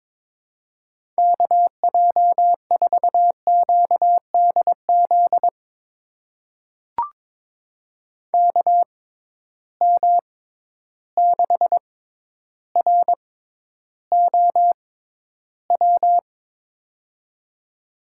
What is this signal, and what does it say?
Morse code, heard as a keyed tone, about 22 wpm, spaced as for 11 wpm, 700 Hz. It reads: KJ4QDZ KM6ROW
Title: USA Call Signs - 6 Characters - 6x Character Spacing - 22wpm